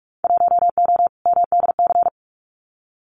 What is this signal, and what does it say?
Morse code, heard as a keyed tone, about 46 wpm, 700 Hz. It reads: 1O MDC
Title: Winter Field Day Exchanges 45wpm